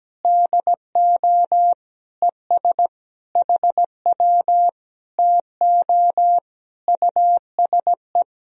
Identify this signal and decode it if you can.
Morse code, heard as a keyed tone, about 17 wpm, 700 Hz. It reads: DO ES HW TO USE
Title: Sentences from Top 500 Words and Abbreviations 17wpm